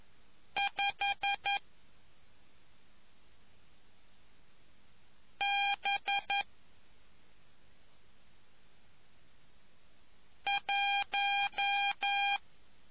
Morse code, heard as a keyed tone, about 11 wpm, 800 Hz. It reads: 5 B 1